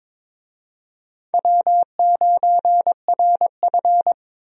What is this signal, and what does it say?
Morse code, heard as a keyed tone, about 22 wpm, 700 Hz. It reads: W9RF
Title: USA Call Signs - 4 Characters - 22wpm